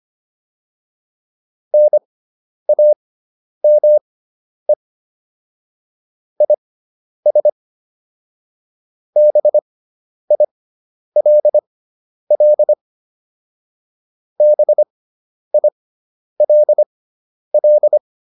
Morse code, heard as a keyed tone, about 25 wpm, spaced as for 10 wpm, 600 Hz. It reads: NAME IS BILL BILL